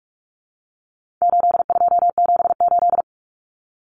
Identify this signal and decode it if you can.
Morse code, heard as a keyed tone, about 44 wpm, 700 Hz. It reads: 8278
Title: Numbers - 4-Digit 45wpm